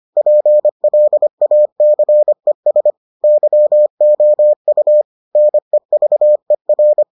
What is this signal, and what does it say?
Morse code, heard as a keyed tone, about 25 wpm, 600 Hz. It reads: PLACES YOU NEVER